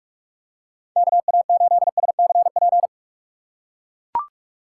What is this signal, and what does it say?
Morse code, heard as a keyed tone, about 44 wpm, 700 Hz. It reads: KA8SCP